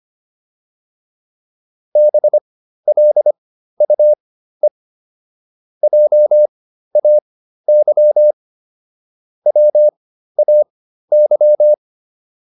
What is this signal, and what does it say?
Morse code, heard as a keyed tone, about 25 wpm, spaced as for 13 wpm, 600 Hz. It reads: BLUE JAY WAY